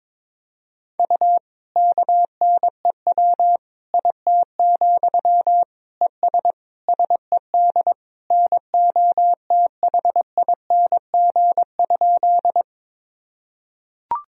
Morse code, heard as a keyed tone, about 22 wpm, 700 Hz. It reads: U KNEW IT, ES SED NOTHING?
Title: Sentences from Top 500 Words and Abbreviations 22wpm